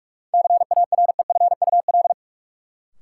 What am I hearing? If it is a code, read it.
Morse code, heard as a keyed tone, about 46 wpm, 700 Hz. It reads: CAREFUL
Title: Sentences from Top 1600 Words - New Word Drill - 45wpm